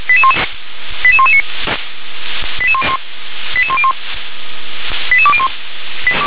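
Five-tone selective call (ZVEI), heard as quick runs of five tones, tones of about 70 ms. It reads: 88119 88101 88201
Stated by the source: ZVEI five-tone selective call received on SDR.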